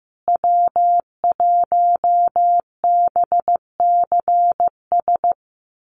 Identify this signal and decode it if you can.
Morse code, heard as a keyed tone, about 15 wpm, 700 Hz. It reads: W1BCS
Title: USA Call Signs Repeated 3x 15wpm